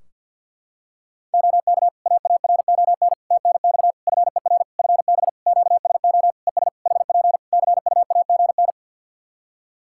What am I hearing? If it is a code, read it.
Morse code, heard as a keyed tone, about 52 wpm, 700 Hz. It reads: OK AARON TNX FER FB QSO ES HP CUAGN
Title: K1USN SST QSO Practice 50wpm